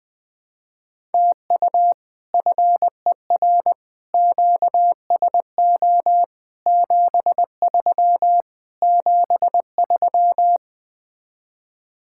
Morse code, heard as a keyed tone, about 20 wpm, 700 Hz. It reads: TU FER QSO 73 73
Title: Speed-Racing Common QSO Phrases - 20wpm